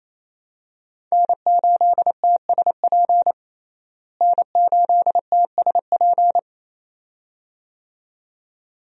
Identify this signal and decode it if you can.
Morse code, heard as a keyed tone, about 28 wpm, 700 Hz. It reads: N8TSP N8TSP
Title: USA Call Signs - 5 Characters - Repeated 2x - 28wpm